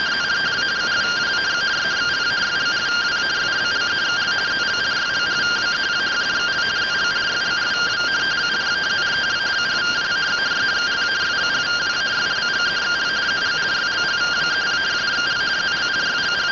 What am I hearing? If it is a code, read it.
Two-tone teleprinter signal, about 170 Hz shift.
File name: findings / CT 1 Scout 0002.gen-23 rtty rx